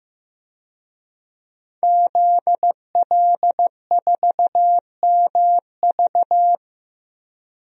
Morse code, heard as a keyed tone, about 15 wpm, 700 Hz. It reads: ZL4MV
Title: DXCC-100 Call Signs - 5 Characters - 15wpm